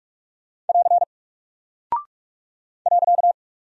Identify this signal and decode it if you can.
Morse code, heard as a keyed tone, about 44 wpm, 700 Hz. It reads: <AR> .
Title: Single Character (Letter:Number:Punctuation:Prosign) Alt 45wpm